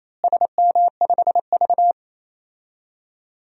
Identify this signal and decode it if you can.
Morse code, heard as a keyed tone, about 28 wpm, 700 Hz. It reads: SM5V
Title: DXCC-100 Call Signs - 4 Characters - Repeated 2x - 28wpm